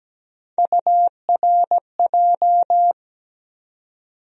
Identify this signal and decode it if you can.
Morse code, heard as a keyed tone, about 17 wpm, 700 Hz. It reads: URJ
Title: USA Call Sign Postfixes - 3 Letters 17wpm